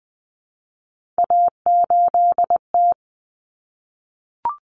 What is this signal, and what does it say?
Morse code, heard as a keyed tone, about 20 wpm, 700 Hz. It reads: A8T
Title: Common Cut Numbers - 3-Digit 20wpm